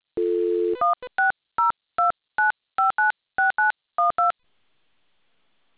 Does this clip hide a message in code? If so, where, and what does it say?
Yes: DTMF from 0.8 to 4.3 s, heard as telephone touch tones, tones of about 150 ms. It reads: 16029596912